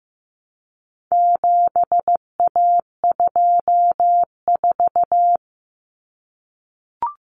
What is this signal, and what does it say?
Morse code, heard as a keyed tone, about 15 wpm, 700 Hz. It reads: 7A24
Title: Common Cut Numbers - 4-Digit 15wpm